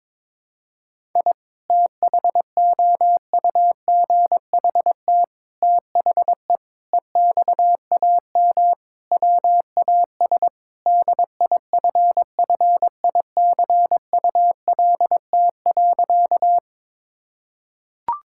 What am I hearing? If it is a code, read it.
Morse code, heard as a keyed tone, about 22 wpm, 700 Hz. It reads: I THOUGHT THE EXAM WAS DIFFICULT.
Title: Sentences from Top 1000 Words - Review Part 5 - 22wpm